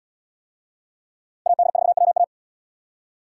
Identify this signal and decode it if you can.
Morse code, heard as a keyed tone, about 73 wpm, 700 Hz. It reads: AD4WA